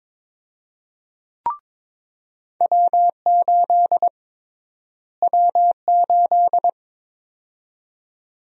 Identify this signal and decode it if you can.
Morse code, heard as a keyed tone, about 22 wpm, 700 Hz. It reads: W8 W8
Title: USA Call Sign Prefixes - Repeated 2x - 22wpm